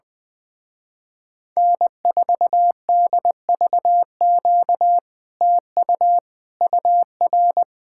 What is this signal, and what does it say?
Morse code, heard as a keyed tone, about 20 wpm, 700 Hz. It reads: N4DVQ TU UR
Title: Speed-Racing Common QSO Phrases - 20wpm